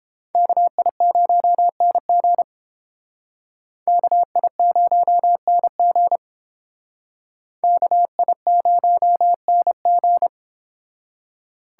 Morse code, heard as a keyed tone, about 29 wpm, 700 Hz. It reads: KI0NG KI0NG KI0NG